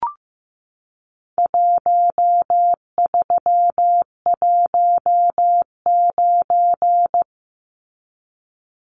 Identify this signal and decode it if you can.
Morse code, heard as a keyed tone, about 15 wpm, 700 Hz. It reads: 1319
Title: Numbers - Contest Style - 4-Digit 15wpm